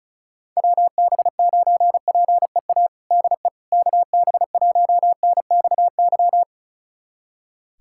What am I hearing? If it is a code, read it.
Morse code, heard as a keyed tone, about 36 wpm, 700 Hz. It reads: WB9PEA DE KB1NXY
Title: USA Callsign DE Callsign - 6 Characters - 35wpm